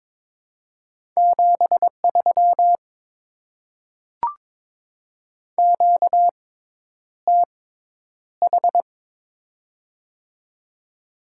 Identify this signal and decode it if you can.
Morse code, heard as a keyed tone, about 22 wpm, spaced as for 10 wpm, 700 Hz. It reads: 73 QTH
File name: Top 20 QSO Elements - 6x Character Spacing - 22wpm